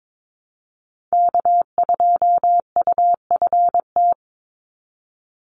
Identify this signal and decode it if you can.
Morse code, heard as a keyed tone, about 22 wpm, 700 Hz. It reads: K2UFT
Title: Call Signs - 22wpm